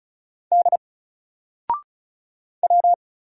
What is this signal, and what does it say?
Morse code, heard as a keyed tone, about 35 wpm, 700 Hz. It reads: D W
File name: Single-Letter-Number-35wpm